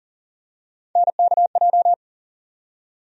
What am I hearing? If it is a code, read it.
Morse code, heard as a keyed tone, about 40 wpm, 700 Hz. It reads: NKJ
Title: USA Call Sign Postfixes - 3 Letters 40wpm